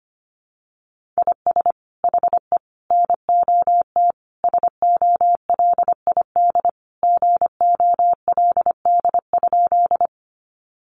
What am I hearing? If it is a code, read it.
Morse code, heard as a keyed tone, about 25 wpm, 700 Hz. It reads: IS HE NOT SOLID GOLD?
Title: Sentences from Top 1500 Words 25wpm